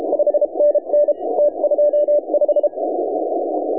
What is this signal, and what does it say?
Morse code, heard as a keyed tone, about 31 wpm, 590 Hz. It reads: 5NN T25
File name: AUDIO2019 / HA7UI CW